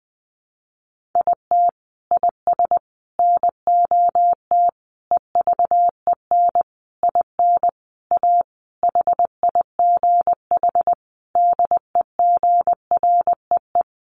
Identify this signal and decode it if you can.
Morse code, heard as a keyed tone, about 19 wpm, 700 Hz. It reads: IT IS NOT EVEN IN A HIGH DEGREE
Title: Sentences from Top 1500 Words 20wpm